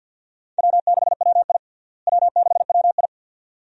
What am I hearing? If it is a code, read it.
Morse code, heard as a keyed tone, about 52 wpm, 700 Hz. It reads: W6WI W6WI
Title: USA Call Signs - 4 Characters - Repeated 2x - 50wpm